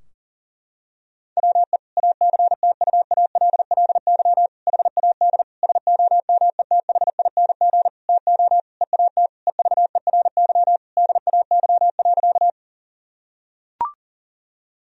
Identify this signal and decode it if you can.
Morse code, heard as a keyed tone, about 40 wpm, 700 Hz. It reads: WE ACTUALLY HAD SOMETHING TO EAT EVERY DAY.
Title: Sentences from Top 1000 Words - Review Part 4 - 40wpm